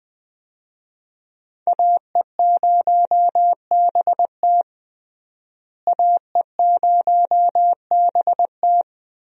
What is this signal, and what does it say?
Morse code, heard as a keyed tone, about 20 wpm, 700 Hz. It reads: AE0BT AE0BT
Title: USA Call Signs - 5 Characters - Repeated 2x - 20wpm